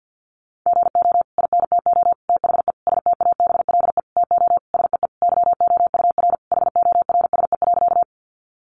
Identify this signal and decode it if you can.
Morse code, heard as a keyed tone, about 50 wpm, 700 Hz. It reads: GO INTO THE STABLE TO SEE YOUR HORSE.
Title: Sentences from Top 1500 Words 50wpm